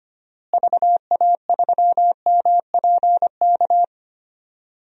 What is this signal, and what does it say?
Morse code, heard as a keyed tone, about 25 wpm, 700 Hz. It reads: VA3MPK
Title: DXCC-100 Call Signs - 6 Characters - Repeated 2x - 25wpm